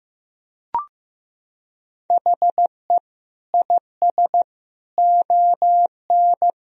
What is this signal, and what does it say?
Morse code, heard as a keyed tone, about 15 wpm, 700 Hz. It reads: HE IS ON
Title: Sentences from Top 1000 Words - Review Part 4 - 15wpm